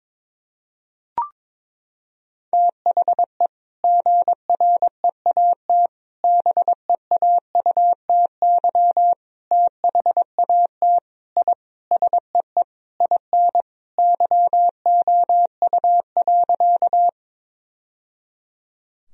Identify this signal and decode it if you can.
Morse code, heard as a keyed tone, about 21 wpm, 700 Hz. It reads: THE GREAT BEAUTY THAT I SEE IN YOU.
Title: Sentences from Top 1000 Words - Review Part 2 - 22wpm